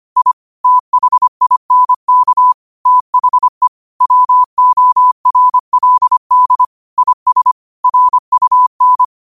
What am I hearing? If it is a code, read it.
Morse code, heard as a keyed tone, about 25 wpm, 1000 Hz. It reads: I THINK THE WORLD IS RUN
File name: qotdfiles / 25wpm